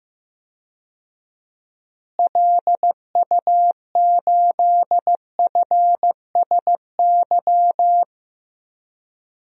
Morse code, heard as a keyed tone, about 15 wpm, 700 Hz. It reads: LU8FSY
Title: DXCC-100 Call Signs - 6 Characters - Repeated 2x - 15wpm